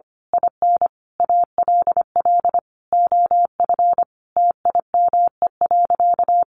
Morse code, heard as a keyed tone, about 25 wpm, 700 Hz. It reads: IN ALL OF TIME.
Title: A Sound Of Thunder 25wpm